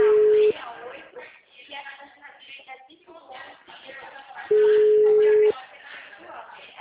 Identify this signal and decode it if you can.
Ringback tone (Europe, 425 Hz), with speech.